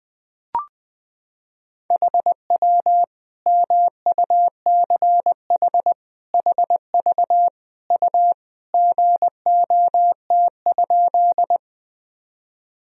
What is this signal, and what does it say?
Morse code, heard as a keyed tone, about 20 wpm, 700 Hz. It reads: HW MUCH HV U GOT?
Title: Sentences from Top 300 Words and Abbreviations 20wpm